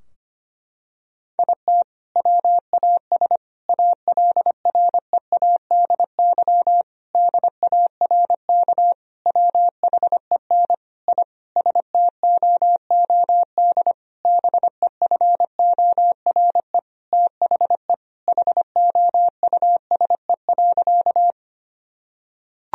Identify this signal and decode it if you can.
Morse code, heard as a keyed tone, about 25 wpm, 700 Hz. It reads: IT WAS ALREADY DARK WHEN I STOOD BEFORE THE HOUSE.
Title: Sentences from Top 1000 Words - Review Part 2 - 25wpm